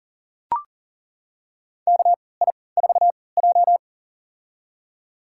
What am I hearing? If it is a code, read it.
Morse code, heard as a keyed tone, about 40 wpm, spaced as for 23 wpm, 700 Hz. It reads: KI4J
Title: USA Call Signs - 4 Characters - 3x Character Spacing - 40wpm